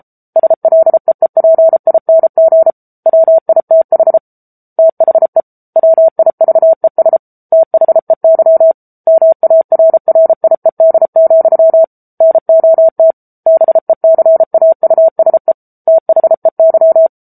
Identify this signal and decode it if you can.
Morse code, heard as a keyed tone, about 34 wpm, 650 Hz. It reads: SLEEPING WITH THE WIVES THEY MARRIED, NOT BECAUSE THEY
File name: Book Two / 07-Chapter 7 - 33 WPM